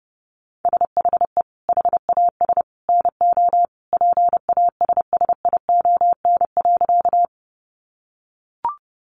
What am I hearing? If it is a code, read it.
Morse code, heard as a keyed tone, about 30 wpm, 700 Hz. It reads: SHE HAS NO PASSION.
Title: Sentences from Top 1300 Words 30wpm